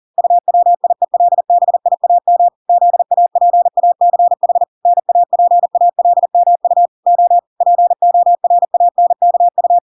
Morse code, heard as a keyed tone, about 40 wpm, 700 Hz. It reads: UWIELBIAM ZAPACH NAPALMU O PORANKU